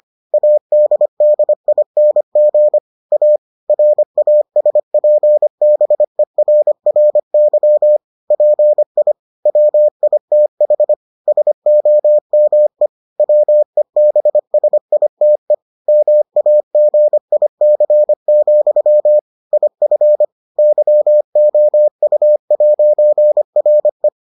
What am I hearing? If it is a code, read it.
Morse code, heard as a keyed tone, about 25 wpm, 600 Hz. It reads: ADDING A RASPBERRY PI WITH SOME WEBSITE MAGIC, IF YOU'RE